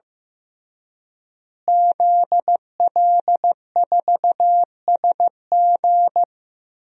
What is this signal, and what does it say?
Morse code, heard as a keyed tone, about 15 wpm, 700 Hz. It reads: ZL4SG
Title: DXCC-100 Call Signs - 5 Characters - Repeated 3x - 15wpm